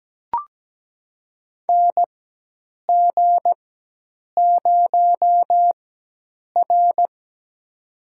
Morse code, heard as a keyed tone, about 17 wpm, spaced as for 8 wpm, 700 Hz. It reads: NG0R
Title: USA Call Signs - 4 Characters - 4x Character Spacing - 17wpm